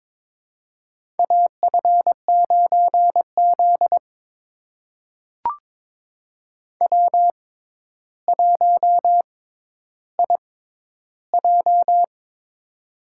Morse code, heard as a keyed tone, about 22 wpm, spaced as for 11 wpm, 700 Hz. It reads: AF9Z W1IJ
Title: USA Call Signs - 4 Characters - 6x Character Spacing - 22wpm